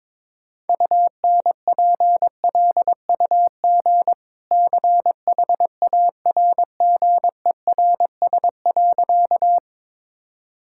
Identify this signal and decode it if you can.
Morse code, heard as a keyed tone, about 22 wpm, 700 Hz. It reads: UNPLUG CHARGERS.
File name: practice / 22wpm / Easy Sentences with Hints - 22wpm